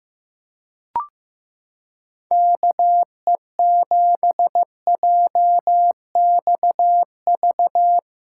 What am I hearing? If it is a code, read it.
Morse code, heard as a keyed tone, about 15 wpm, 700 Hz. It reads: KE7JXV
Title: USA Callsign DE Callsign 15wpm